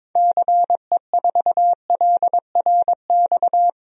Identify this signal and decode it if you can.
Morse code, heard as a keyed tone, about 22 wpm, 700 Hz. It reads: CE4LRX
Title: DXCC-100 Call Signs - 6 Characters - 22wpm